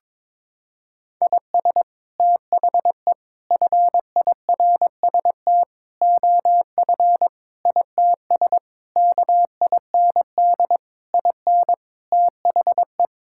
Morse code, heard as a keyed tone, about 22 wpm, 700 Hz. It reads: IS THE FIRST OF ITS KIND IN THE
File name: practice / 22wpm / Sets of 8 Words - 22wpm